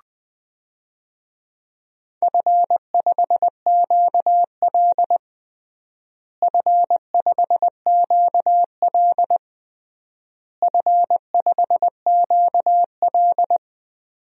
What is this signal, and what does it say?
Morse code, heard as a keyed tone, about 20 wpm, 700 Hz. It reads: F5QL F5QL F5QL
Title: DXCC-100 Call Signs - 4 Characters - Repeated 3x - 20wpm